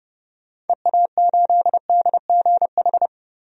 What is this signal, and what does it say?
Morse code, heard as a keyed tone, about 30 wpm, 700 Hz. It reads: EA8DGH
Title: DXCC-100 Call Signs - 6 Characters - 30wpm